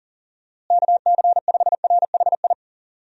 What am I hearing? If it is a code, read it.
Morse code, heard as a keyed tone, about 40 wpm, 700 Hz. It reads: KC5RHI
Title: USA Call Signs - 6 Characters - Repeated 3x - 40wpm